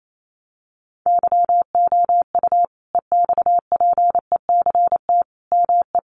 Morse code, heard as a keyed tone, about 28 wpm, 700 Hz. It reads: YOU EXPECT ME
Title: Sentences from Top 1100 Words 28wpm